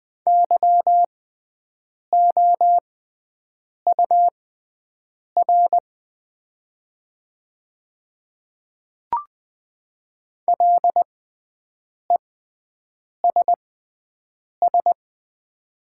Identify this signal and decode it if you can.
Morse code, heard as a keyed tone, about 20 wpm, spaced as for 7 wpm, 700 Hz. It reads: YOUR LESS
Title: 6x Character Spacing - Top 500 Words - 20wpm